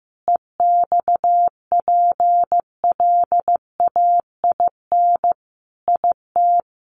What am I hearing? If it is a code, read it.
Morse code, heard as a keyed tone, about 15 wpm, 700 Hz. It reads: EXPLAIN IT
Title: Sentences from Top 1100 Words 15wpm